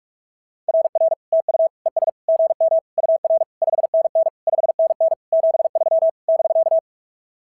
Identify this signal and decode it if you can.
Morse code, heard as a keyed tone, about 46 wpm, 690 Hz. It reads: RR TU ES GM UR 5NN 5NN 73 <BK>
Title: SOTA QSO Practice 45wpm